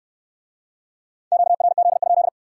Morse code, heard as a keyed tone, about 60 wpm, 700 Hz. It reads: QRZ?